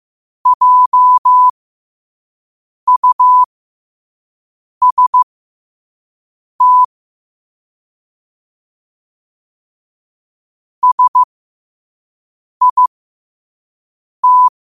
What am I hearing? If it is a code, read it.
Morse code, heard as a keyed tone, about 15 wpm, spaced as for 5 wpm, 1000 Hz. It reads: JUST SIT
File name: qotdfiles / 5wpm